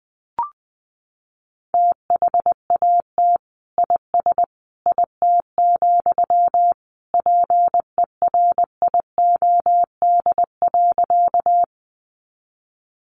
Morse code, heard as a keyed tone, about 20 wpm, 700 Hz. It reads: THAT IS IT, PERIOD.
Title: Sentences from Top 800 Words 20wpm